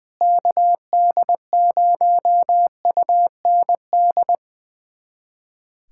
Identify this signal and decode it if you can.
Morse code, heard as a keyed tone, about 20 wpm, 700 Hz. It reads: KD0UND
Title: USA Callsign DE Callsign - 6 Characters - 20wpm